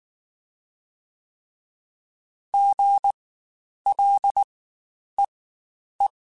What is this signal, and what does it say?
Morse code, heard as a keyed tone, about 19 wpm, spaced as for 9 wpm, 800 Hz. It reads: GLEE
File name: Spaced Alphas / Spaced Alphas mp3s / Gs sp @18